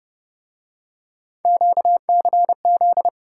Morse code, heard as a keyed tone, about 30 wpm, 700 Hz. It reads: QCZ